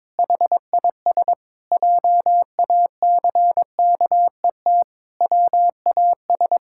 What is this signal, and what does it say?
Morse code, heard as a keyed tone, about 22 wpm, 700 Hz. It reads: HIS JACKET WAS
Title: Pangram Challenge v2 - 22wpm